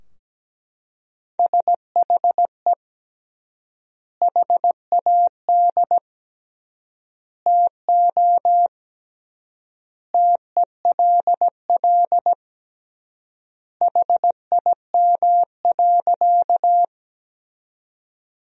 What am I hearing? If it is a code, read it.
Morse code, heard as a keyed tone, about 17 wpm, 700 Hz. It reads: SHE HAD TO TELL HIM.
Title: 3x Word-Spacing - Sentences from Top 200 Words - 17wpm